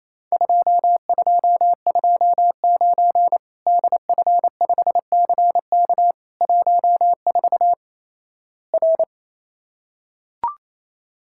Morse code, heard as a keyed tone, about 28 wpm, 700 Hz. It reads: 2229 DF5CK 14 R
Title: Worked All Europe DX Contest - QTC Exchange - 28wpm